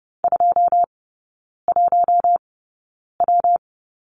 Morse code, heard as a keyed tone, about 30 wpm, spaced as for 9 wpm, 700 Hz. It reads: 21W
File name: Lesson 19e - Letter w - Three Characters 30wpm